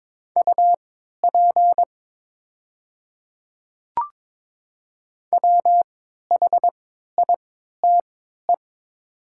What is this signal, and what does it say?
Morse code, heard as a keyed tone, about 22 wpm, spaced as for 13 wpm, 700 Hz. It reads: UP WHITE
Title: 3x Character Spacing - Top 300 Words - 22wpm